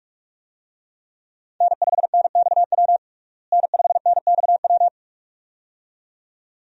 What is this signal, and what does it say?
Morse code, heard as a keyed tone, about 46 wpm, 700 Hz. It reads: N5NXW N5NXW
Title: USA Call Signs - 5 Characters - Repeated 2x - 45wpm